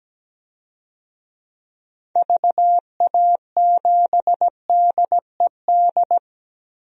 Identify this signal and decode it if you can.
Morse code, heard as a keyed tone, about 17 wpm, 700 Hz. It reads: VA7DED
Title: DXCC-100 Call Signs - 6 Characters - 17wpm